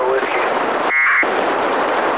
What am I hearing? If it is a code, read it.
Packet radio (AX.25, as in APRS), with speech.
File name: ho-68-12-24-2009